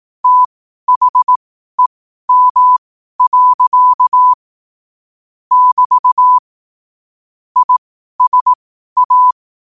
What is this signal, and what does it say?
Morse code, heard as a keyed tone, about 18 wpm, spaced as for 12 wpm, 1000 Hz. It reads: THEM. <BT> ISA
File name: qotdfiles / 12wpm